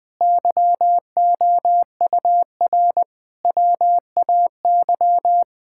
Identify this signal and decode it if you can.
Morse code, heard as a keyed tone, about 20 wpm, 700 Hz. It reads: YOUR WAY
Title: Sentences from Top 1600 Words 20wpm